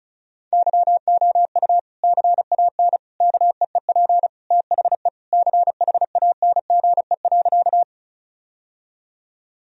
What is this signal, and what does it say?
Morse code, heard as a keyed tone, about 36 wpm, 700 Hz. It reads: YOU CAN KEEP THE CHANGE.
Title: Sentences from Top 1000 Words - Review Part 1 - 35wpm